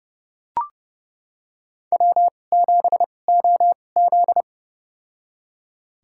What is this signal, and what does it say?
Morse code, heard as a keyed tone, about 30 wpm, spaced as for 22 wpm, 700 Hz. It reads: W7OZ
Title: USA Call Signs - 4 Characters - 2x Character Spacing - 30wpm